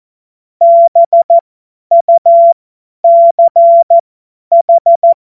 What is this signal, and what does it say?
Morse code, heard as a keyed tone, about 14 wpm, spaced as for 10 wpm, 680 Hz. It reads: BUCH